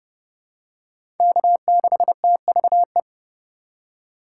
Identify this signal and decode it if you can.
Morse code, heard as a keyed tone, about 31 wpm, 700 Hz. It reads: K6TVE
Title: USA Call Signs - 4x Character Spacing - 30wpm